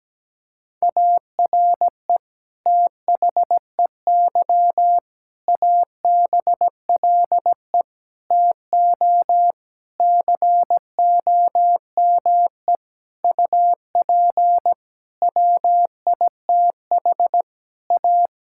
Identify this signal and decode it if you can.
Morse code, heard as a keyed tone, about 17 wpm, 700 Hz. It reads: ARE THEY ABLE TO COME UP WITH A
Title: Sentences from Top 1000 Words - Review Part 5 - 17wpm